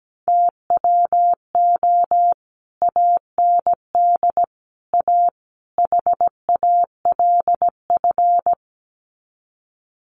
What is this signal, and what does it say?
Morse code, heard as a keyed tone, about 17 wpm, 700 Hz. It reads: TWO AND A HALF
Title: Sets of 4 Words 17wpm